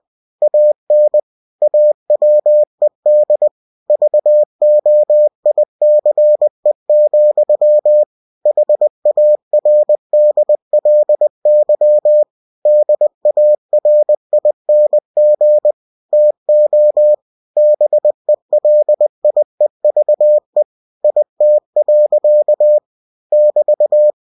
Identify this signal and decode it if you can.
Morse code, heard as a keyed tone, about 20 wpm, 600 Hz. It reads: AN AWED VOICE, HARDLY DARING TO BELIEVE IT. <BT>
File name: cw-books / winnie-the-pooh / winnie-20wpm-page-0004